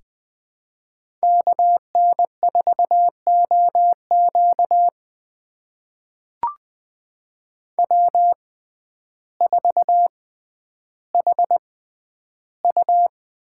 Morse code, heard as a keyed tone, about 20 wpm, spaced as for 11 wpm, 700 Hz. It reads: KN4OQ W4HU
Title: USA Call Signs - 5 Characters - 6x Character Spacing - 20wpm